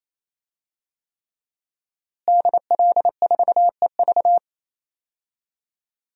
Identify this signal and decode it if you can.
Morse code, heard as a keyed tone, about 28 wpm, 700 Hz. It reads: DL4EV
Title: DXCC-100 Call Signs - 5 Characters - Repeated 3x - 28wpm